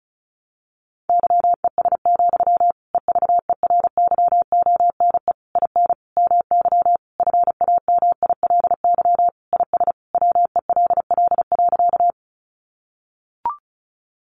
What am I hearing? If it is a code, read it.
Morse code, heard as a keyed tone, about 34 wpm, 700 Hz. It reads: YES, EVERYONE IN MY FAMILY IS WELL.
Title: Sentences from Top 800 Words 35wpm